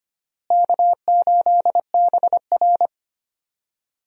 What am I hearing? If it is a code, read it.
Morse code, heard as a keyed tone, about 25 wpm, 700 Hz. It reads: K8BR
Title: USA Call Signs - 4 Characters - Repeated 3x - 25wpm